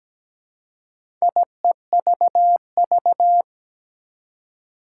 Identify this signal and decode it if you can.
Morse code, heard as a keyed tone, about 17 wpm, 700 Hz. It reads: IEVV
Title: Four Challenging Letter-Number Combinations 17wpm